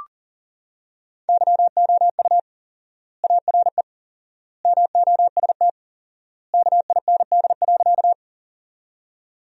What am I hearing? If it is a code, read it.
Morse code, heard as a keyed tone, about 40 wpm, 700 Hz. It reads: YOU ARE MOST KIND.
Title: 4x Word-Spacing - Sentences from Top 200 Words - 40wpm